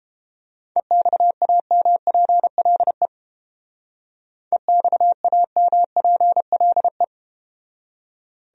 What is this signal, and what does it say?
Morse code, heard as a keyed tone, about 31 wpm, 700 Hz. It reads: EXAMPLE EXAMPLE